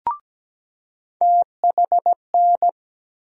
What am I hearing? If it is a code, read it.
Morse code, heard as a keyed tone, about 17 wpm, 700 Hz. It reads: THN
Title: USA Call Sign Postfixes - 3 Letters 17wpm